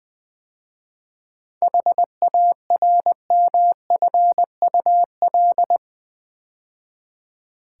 Morse code, heard as a keyed tone, about 20 wpm, 700 Hz. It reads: HARMFUL
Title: 7 Letter Words - First 3 Letters Given - 20wpm